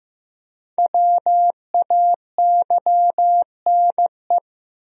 Morse code, heard as a keyed tone, about 15 wpm, 700 Hz. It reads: WAYNE
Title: USA Common Names - Max 5 Letters - 15wpm